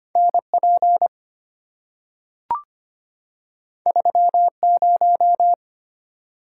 Morse code, heard as a keyed tone, about 25 wpm, 700 Hz. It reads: N P 3 0
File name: Two Letter-Number Combinations - 25wpm